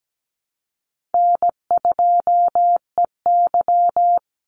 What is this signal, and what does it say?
Morse code, heard as a keyed tone, about 17 wpm, 700 Hz. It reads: N2EY
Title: Easy Call Signs Repeated 3x - 17wpm